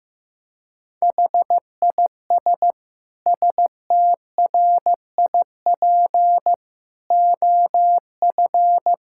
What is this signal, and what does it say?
Morse code, heard as a keyed tone, about 15 wpm, 700 Hz. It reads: HIS STRIP OF
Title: Sentences from Top 1600 Words 15wpm